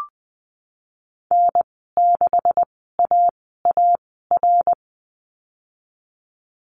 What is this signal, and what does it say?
Morse code, heard as a keyed tone, about 19 wpm, spaced as for 14 wpm, 700 Hz. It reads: N6AAR
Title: 173 Lesson 28g - Number 9 - Call Signs 20wpm